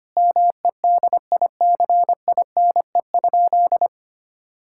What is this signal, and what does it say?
Morse code, heard as a keyed tone, about 25 wpm, 700 Hz. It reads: MEDICINE?
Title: Sentences from Top 1000 Words - Review Part 6 - 25wpm